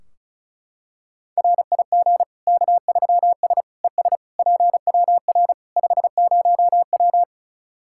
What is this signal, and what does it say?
Morse code, heard as a keyed tone, about 36 wpm, 700 Hz. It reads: RIG K3S ES PWR 50W
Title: Practice QSO - Exchange 2 - Info - 35wpm